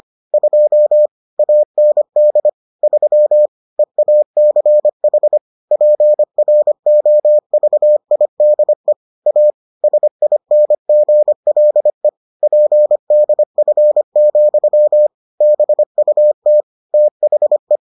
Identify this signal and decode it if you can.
Morse code, heard as a keyed tone, about 25 wpm, 600 Hz. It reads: 2 AND 3 EACH PROVIDE A SINGLE PDF, BUT THE